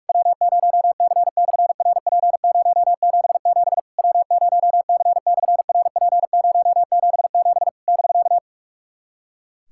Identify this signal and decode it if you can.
Morse code, heard as a keyed tone, about 46 wpm, 700 Hz. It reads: W0C/RP077 W0C/RP077 <BK>
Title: SOTA QSO Practice 45wpm